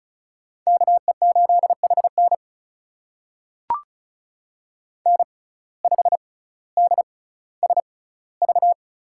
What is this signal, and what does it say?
Morse code, heard as a keyed tone, about 36 wpm, spaced as for 18 wpm, 700 Hz. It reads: KE8HN N5DSV
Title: USA Call Signs - 5 Characters - 6x Character Spacing - 35wpm